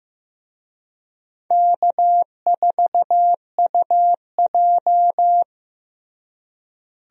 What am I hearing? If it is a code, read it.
Morse code, heard as a keyed tone, about 16 wpm, 700 Hz. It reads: K4UJ
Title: USA Call Signs - 4 Characters - Repeated 3x - 15wpm